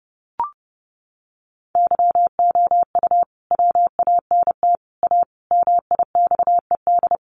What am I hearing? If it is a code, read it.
Morse code, heard as a keyed tone, about 30 wpm, 700 Hz. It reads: YOU WANT A MIXED
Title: Sentences from Top 1500 Words 30wpm